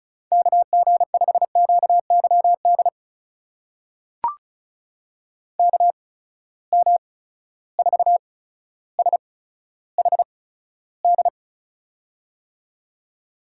Morse code, heard as a keyed tone, about 36 wpm, spaced as for 15 wpm, 700 Hz. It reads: KG5QYD KM4SHD
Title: USA Call Signs - 6 Characters - 8x Character Spacing - 35wpm